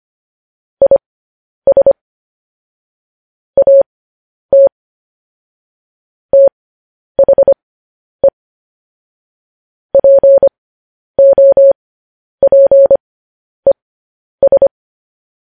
Morse code, heard as a keyed tone, about 25 wpm, spaced as for 10 wpm, 550 Hz. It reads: IS AT THE POPES
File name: podcast / News headlines-1x-25x10